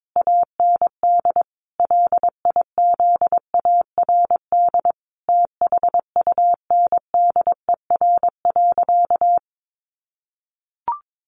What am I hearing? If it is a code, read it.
Morse code, heard as a keyed tone, about 22 wpm, 700 Hz. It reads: AND LIZARD THUNDER.
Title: A Sound Of Thunder 22wpm